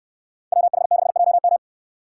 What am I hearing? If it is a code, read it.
Morse code, heard as a keyed tone, about 64 wpm, 700 Hz. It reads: WH6JM